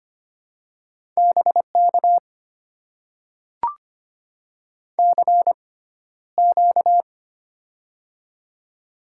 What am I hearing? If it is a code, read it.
Morse code, heard as a keyed tone, about 25 wpm, spaced as for 13 wpm, 700 Hz. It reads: BK CQ
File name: Top 90 QSO Elements - 6x Character Spacing - 25wpm